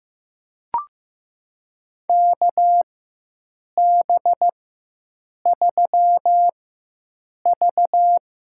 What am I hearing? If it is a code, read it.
Morse code, heard as a keyed tone, about 15 wpm, spaced as for 7 wpm, 700 Hz. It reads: KB3V
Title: USA Call Signs - 4x Character Spacing - 15wpm